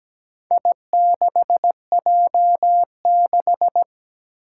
Morse code, heard as a keyed tone, about 17 wpm, 700 Hz. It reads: I6J6
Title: Four Challenging Letter-Number Combinations 17wpm